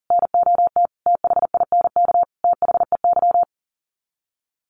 Morse code, heard as a keyed tone, about 40 wpm, 700 Hz. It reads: NOT THINK THEY
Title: Sets of 5 Words 40wpm